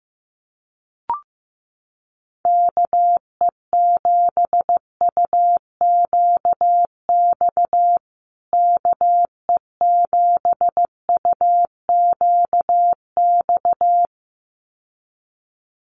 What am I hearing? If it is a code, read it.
Morse code, heard as a keyed tone, about 15 wpm, 700 Hz. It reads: KE7UQX KE7UQX
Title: USA Call Signs Repeated 2x 15wpm